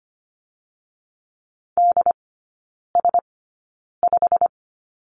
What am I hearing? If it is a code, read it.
Morse code, heard as a keyed tone, about 25 wpm, spaced as for 9 wpm, 690 Hz. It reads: DS5
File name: Lesson 15e - Number 5 - Three Characters 25wpm